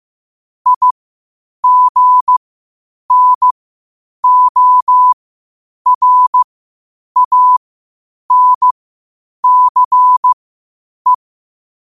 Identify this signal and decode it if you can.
Morse code, heard as a keyed tone, about 15 wpm, spaced as for 8 wpm, 1000 Hz. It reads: IGNORANCE
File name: qotdfiles / 8wpm